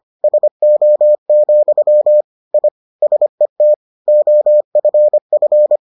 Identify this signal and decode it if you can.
Morse code, heard as a keyed tone, about 25 wpm, 600 Hz. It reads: SO, I SET OFF